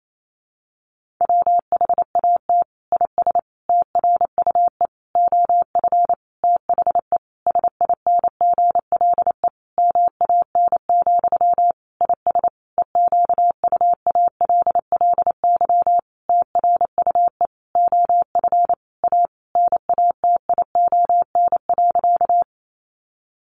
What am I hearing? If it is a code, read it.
Morse code, heard as a keyed tone, about 28 wpm, 700 Hz. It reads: WHAT IS TRUE OF THE SINGLE MAN, IS EQUALLY TRUE OF A NATION.
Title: Sentences from Top 1400 Words 28wpm